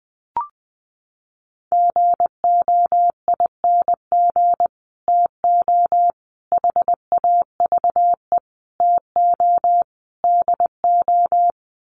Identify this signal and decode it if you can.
Morse code, heard as a keyed tone, about 20 wpm, 700 Hz. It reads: GOING TO HAVE TO DO
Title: Sets of 5 Words Encore 20wpm